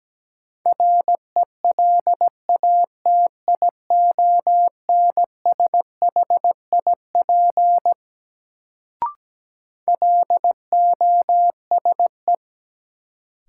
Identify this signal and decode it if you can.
Morse code, heard as a keyed tone, about 17 wpm, 700 Hz. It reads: RELATIONSHIP LOSE
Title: Sentences from Top 1100 Words - New Word Drill Alt - 17wpm